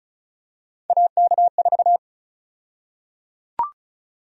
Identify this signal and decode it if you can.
Morse code, heard as a keyed tone, about 36 wpm, 700 Hz. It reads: AK4
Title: USA Call Sign Prefixes - 35wpm